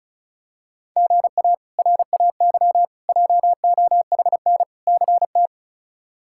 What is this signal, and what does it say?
Morse code, heard as a keyed tone, about 36 wpm, 700 Hz. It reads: GA RAY JOHN CT
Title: K1USN SST QSO Practice - 35wpm